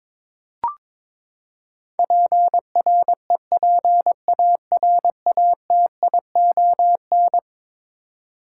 Morse code, heard as a keyed tone, about 21 wpm, 700 Hz. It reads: PREPARATION
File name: practice / 22wpm / Eleven Letter Words - 22wpm